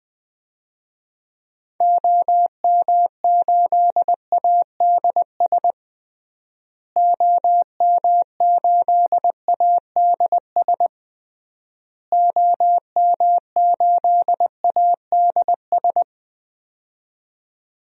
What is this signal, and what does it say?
Morse code, heard as a keyed tone, about 21 wpm, 700 Hz. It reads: OM8ADS OM8ADS OM8ADS
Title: DXCC-100 Call Signs - 6 Characters - Repeated 3x - 20wpm